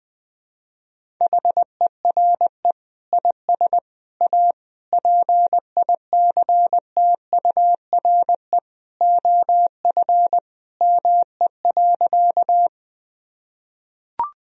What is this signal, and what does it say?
Morse code, heard as a keyed tone, about 20 wpm, 700 Hz. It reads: HERE IS A PICTURE OF ME.
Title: Sentences from Top 1000 Words - Review Part 1 - 20wpm